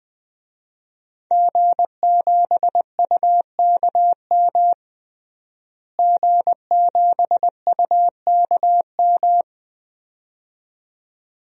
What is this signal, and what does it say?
Morse code, heard as a keyed tone, about 21 wpm, 700 Hz. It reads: G7UKM G7UKM
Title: DXCC-100 Call Signs - 5 Characters - Repeated 2x - 20wpm